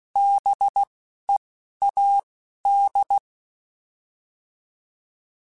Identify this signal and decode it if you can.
Morse code, heard as a keyed tone, about 16 wpm, 800 Hz. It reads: B E A D